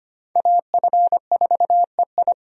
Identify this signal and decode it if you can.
Morse code, heard as a keyed tone, about 25 wpm, 700 Hz. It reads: AF4EI
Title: USA Call Signs - 5 Characters - 25wpm